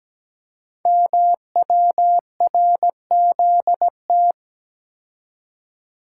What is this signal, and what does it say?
Morse code, heard as a keyed tone, about 17 wpm, 700 Hz. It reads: MWRZT
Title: Code Groups - 5 Letters - 17wpm